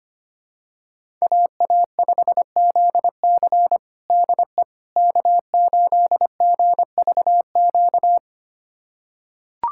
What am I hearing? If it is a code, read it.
Morse code, heard as a keyed tone, about 25 wpm, 700 Hz. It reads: AA5ZC DE K8GVQ
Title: USA Callsign DE Callsign - 5 Characters - 25wpm